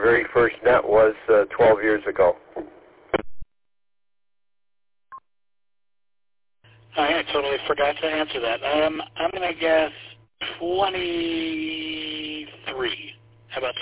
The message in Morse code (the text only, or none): none